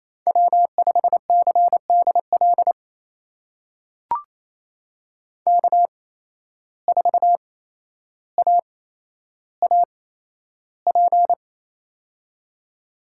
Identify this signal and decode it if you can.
Morse code, heard as a keyed tone, about 28 wpm, spaced as for 12 wpm, 700 Hz. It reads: W5CDL K4AAP
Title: USA Call Signs - 5 Characters - 8x Character Spacing - 28wpm